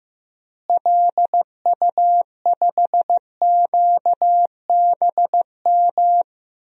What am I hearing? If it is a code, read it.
Morse code, heard as a keyed tone, about 15 wpm, 700 Hz. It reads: LU5QBM
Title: DXCC-100 Call Signs - 6 Characters - Repeated 2x - 15wpm